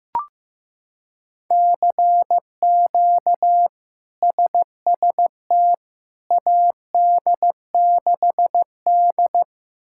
Morse code, heard as a keyed tone, about 15 wpm, 700 Hz. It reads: CQ SST AD6D
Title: K1USN SST QSO Practice - 15wpm